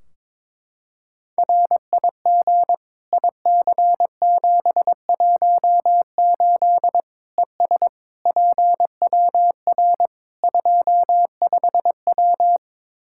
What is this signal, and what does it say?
Morse code, heard as a keyed tone, about 22 wpm, 700 Hz. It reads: RIG IC718 ES PWR 25W
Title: Practice QSO - Exchange 2 - Info - 22wpm